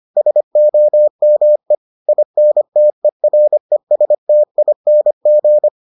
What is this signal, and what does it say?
Morse code, heard as a keyed tone, about 25 wpm, 600 Hz. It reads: SOME INTERESTING